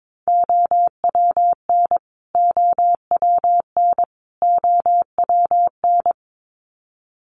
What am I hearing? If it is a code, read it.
Morse code, heard as a keyed tone, about 22 wpm, 700 Hz. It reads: OWN OWN OWN
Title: Top 500 Words Repeated 22wpm